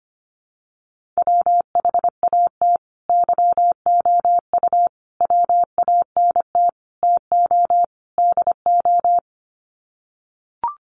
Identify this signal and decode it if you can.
Morse code, heard as a keyed tone, about 25 wpm, 700 Hz. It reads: WHAT YOU WANT TO DO
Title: Sets of 5 Words Encore 25wpm